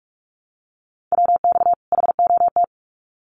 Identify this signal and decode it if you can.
Morse code, heard as a keyed tone, about 44 wpm, 700 Hz. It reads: WX HOT
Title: Common QSO Phrases 45wpm